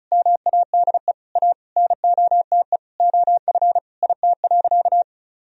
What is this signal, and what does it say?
Morse code, heard as a keyed tone, about 34 wpm, 700 Hz. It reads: MADE A NOTE OF IT.
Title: Sentences from Top 1000 Words - Review Part 2 - 35wpm